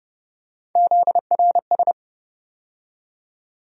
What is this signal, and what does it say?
Morse code, heard as a keyed tone, about 30 wpm, 700 Hz. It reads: ZRS
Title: USA Call Sign Postfixes - 3 Letters 30wpm